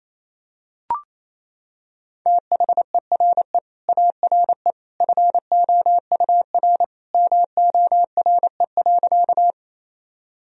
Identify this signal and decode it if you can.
Morse code, heard as a keyed tone, about 27 wpm, 700 Hz. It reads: THERE ARE FOUR MORE.
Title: Sentences from Top 1000 Words - Review Part 1 - 28wpm